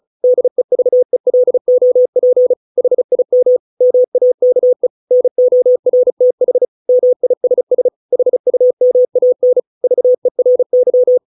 Morse code, heard as a keyed tone, about 34 wpm, 500 Hz. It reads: DEVELOP HIM MAKE NORTH MISS HUMAN VERY